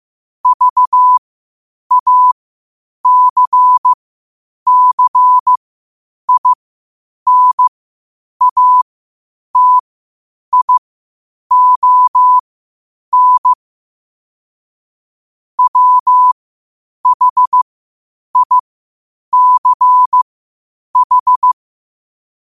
Morse code, heard as a keyed tone, about 15 wpm, spaced as for 8 wpm, 1000 Hz. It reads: VACCINATION WHICH
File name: qotdfiles / 8wpm